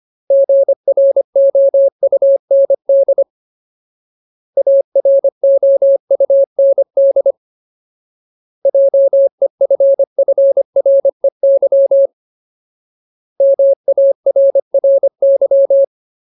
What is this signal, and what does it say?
Morse code, heard as a keyed tone, about 25 wpm, 550 Hz. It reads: GROUND AROUND JEFFREY MARRY
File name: Ultimate Word List / 25WPM Ultimate